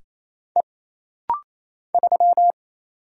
Morse code, heard as a keyed tone, about 28 wpm, 700 Hz. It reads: E 3
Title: Single Letter-Number - Rapid-Fire 28wpm